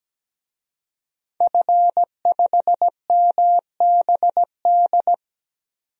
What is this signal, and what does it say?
Morse code, heard as a keyed tone, about 17 wpm, 700 Hz. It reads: F5MBD
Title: DXCC-100 Call Signs - 5 Characters - 17wpm